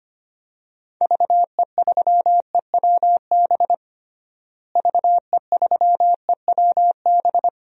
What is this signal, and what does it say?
Morse code, heard as a keyed tone, about 25 wpm, 700 Hz. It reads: VE3EWB VE3EWB
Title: DXCC-100 Call Signs - 6 Characters - Repeated 2x - 25wpm